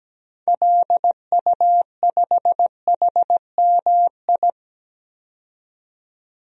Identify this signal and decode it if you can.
Morse code, heard as a keyed tone, about 17 wpm, 700 Hz. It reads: LU5HMI
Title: DXCC-100 Call Signs - 6 Characters - 17wpm